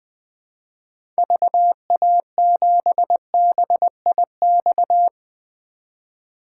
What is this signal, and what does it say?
Morse code, heard as a keyed tone, about 21 wpm, 700 Hz. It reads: VA7BIX
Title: 2023 POTA Activator Call Signs - 20wpm